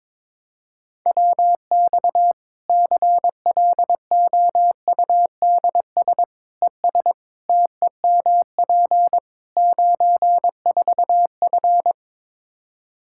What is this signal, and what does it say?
Morse code, heard as a keyed tone, about 21 wpm, 700 Hz. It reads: WX CLOUDS ES TEMP 94F
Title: K1USN SST QSO Practice 22wpm